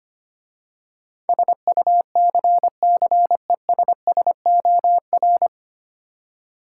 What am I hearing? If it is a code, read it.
Morse code, heard as a keyed tone, about 25 wpm, 700 Hz. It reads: SUCCESSOR
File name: practice / 25wpm / 9 Letter Words - First 3 Letters Given - 25wpm